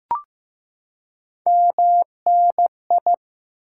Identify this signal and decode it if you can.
Morse code, heard as a keyed tone, about 15 wpm, 700 Hz. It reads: MNI
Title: Top 90 QSO Elements 15wpm